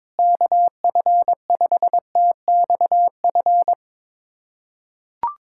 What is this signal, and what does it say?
Morse code, heard as a keyed tone, about 22 wpm, 700 Hz. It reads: KF5TXF
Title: USA Callsign DE Callsign 22wpm